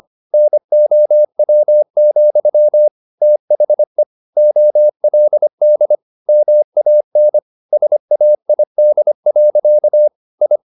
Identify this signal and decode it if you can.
Morse code, heard as a keyed tone, about 25 wpm, 600 Hz. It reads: NOW, THE OLD MAN SAID. I
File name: the old man and the sea part 1 25wpm